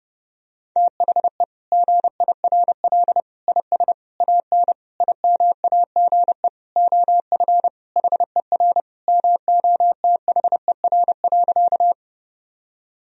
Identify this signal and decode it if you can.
Morse code, heard as a keyed tone, about 30 wpm, 700 Hz. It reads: THE GIRL IS AN IMAGE OF HER MOTHER.
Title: Sentences from Top 1000 Words - Review Part 4 - 30wpm